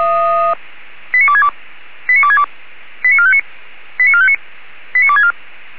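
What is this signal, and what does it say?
Five-tone selective call (ZVEI), heard as quick runs of five tones, tones of about 70 ms. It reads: 89262 89262 89459 89459 89264
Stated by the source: ZVEI five-tone selective call received by SDR.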